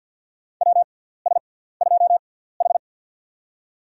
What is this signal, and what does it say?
Morse code, heard as a keyed tone, about 51 wpm, spaced as for 17 wpm, 700 Hz. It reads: WS2H
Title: USA Call Signs - 4 Characters - 6x Character Spacing - 50wpm